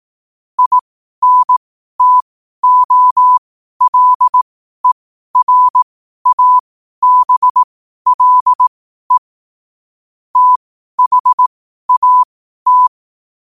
Morse code, heard as a keyed tone, about 18 wpm, spaced as for 12 wpm, 1000 Hz. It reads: INTOLERABLE THAT
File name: qotdfiles / 12wpm